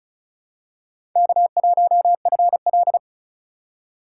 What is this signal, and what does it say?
Morse code, heard as a keyed tone, about 34 wpm, 700 Hz. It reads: K1FL
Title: USA Call Signs - 4x Character Spacing - 35wpm